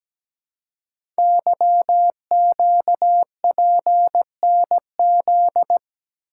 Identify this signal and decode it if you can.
Morse code, heard as a keyed tone, about 17 wpm, 700 Hz. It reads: YQPNZ
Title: Code Groups - 5 Letters - 17wpm